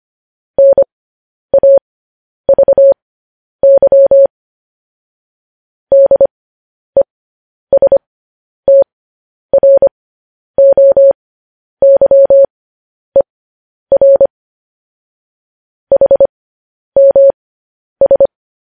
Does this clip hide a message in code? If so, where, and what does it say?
Yes: Morse code from 0.6 to 18.3 s, heard as a keyed tone, about 25 wpm, spaced as for 10 wpm, 550 Hz. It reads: NAVY DESTROYER HMS